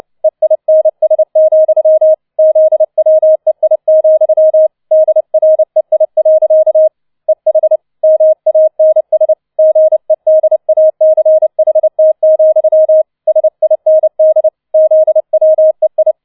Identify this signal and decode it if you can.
Morse code, heard as a keyed tone, about 29 wpm, 630 Hz. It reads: EINS, ZWEI, DREI. EH MANS GEDACHT, SIND ZWEI
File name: 006 Sechster Streich 30wpm